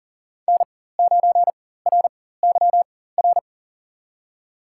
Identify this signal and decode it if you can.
Morse code, heard as a keyed tone, about 40 wpm, spaced as for 19 wpm, 700 Hz. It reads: N9RYR
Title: USA Call Signs - 5 Characters - 4x Character Spacing - 40wpm